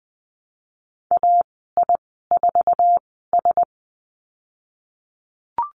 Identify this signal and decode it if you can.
Morse code, heard as a keyed tone, about 20 wpm, spaced as for 14 wpm, 700 Hz. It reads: AI4S
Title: 055 Lesson 10g - Letter r - Call Signs 20wpm